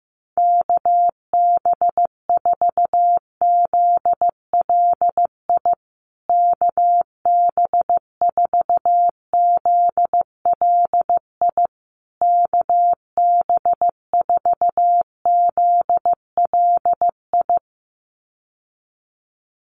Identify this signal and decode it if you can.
Morse code, heard as a keyed tone, about 15 wpm, 700 Hz. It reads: KB4ZLI KB4ZLI KB4ZLI
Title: USA Call Signs Repeated 3x 15wpm